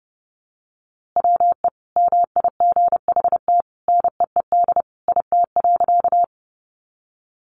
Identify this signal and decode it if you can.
Morse code, heard as a keyed tone, about 30 wpm, 700 Hz. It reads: WE MIGHT NEED IT.
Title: A Sound Of Thunder 30wpm